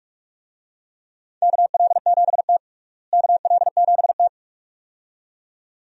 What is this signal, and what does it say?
Morse code, heard as a keyed tone, about 42 wpm, 700 Hz. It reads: KL7T KL7T
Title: DXCC-100 Call Signs - 4 Characters - Repeated 2x - 45wpm